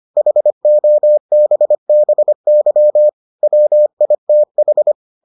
Morse code, heard as a keyed tone, about 25 wpm, 600 Hz. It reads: HOBBY WITH